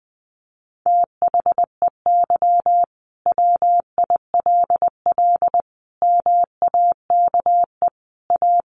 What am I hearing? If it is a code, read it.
Morse code, heard as a keyed tone, about 20 wpm, 700 Hz. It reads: THEY WILL MAKE A
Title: Sentences from Top 1500 Words 20wpm